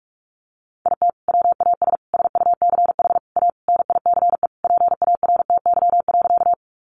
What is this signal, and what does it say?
Morse code, heard as a keyed tone, about 44 wpm, 700 Hz. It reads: IT WAS SUCH A NICE PARTY.
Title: Sentences from Top 1400 Words 45wpm